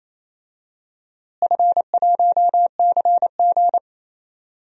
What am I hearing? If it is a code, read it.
Morse code, heard as a keyed tone, about 27 wpm, 700 Hz. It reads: F1CG
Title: DXCC-100 Call Signs - 4 Characters - Repeated 2x - 28wpm